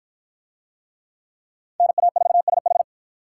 Morse code, heard as a keyed tone, about 53 wpm, 700 Hz. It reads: NA4SH